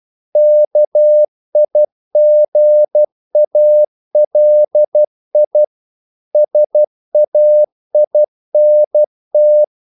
Morse code, heard as a keyed tone, about 12 wpm, 600 Hz. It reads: KIGALI SAINT
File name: CW / Practice / NationalCapitals12